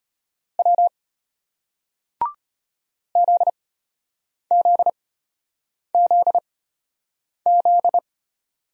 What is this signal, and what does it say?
Morse code, heard as a keyed tone, about 34 wpm, spaced as for 8 wpm, 700 Hz. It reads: W ZZZZ